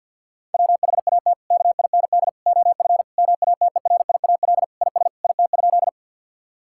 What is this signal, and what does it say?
Morse code, heard as a keyed tone, about 50 wpm, 700 Hz. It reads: WHAT KIND OF MATERIAL IS IT?
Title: Sentences from Top 1000 Words - Review Part 5 - 50wpm